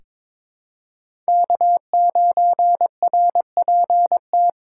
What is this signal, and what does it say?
Morse code, heard as a keyed tone, about 22 wpm, 700 Hz. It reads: K9RPT
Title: USA Call Signs - 2x Character Spacing - 22wpm